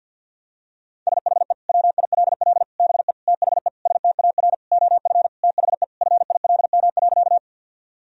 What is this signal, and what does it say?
Morse code, heard as a keyed tone, about 50 wpm, 700 Hz. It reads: SHE WILL BE THE STAR OF THE FILM.
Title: Sentences from Top 1000 Words - Review Part 3 - 50wpm